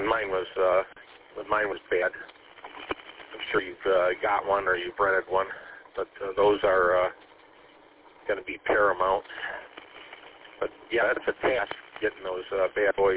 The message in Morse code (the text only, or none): none